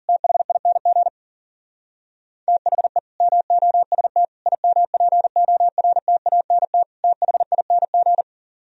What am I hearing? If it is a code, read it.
Morse code, heard as a keyed tone, about 51 wpm, 700 Hz. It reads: THING THE MOST IMPORTANT THING